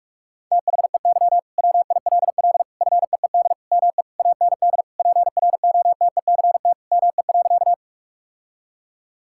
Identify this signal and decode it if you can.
Morse code, heard as a keyed tone, about 41 wpm, 700 Hz. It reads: THEY WILL FEED ME AND PROTECT ME.
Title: Sentences from Top 1600 Words 45wpm